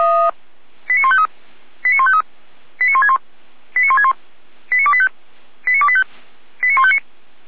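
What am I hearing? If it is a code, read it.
Five-tone selective call (ZVEI), heard as quick runs of five tones, tones of about 70 ms. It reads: 89153 89153 89161 89161 89285 89285 89159
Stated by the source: ZVEI five-tone call received on SDR.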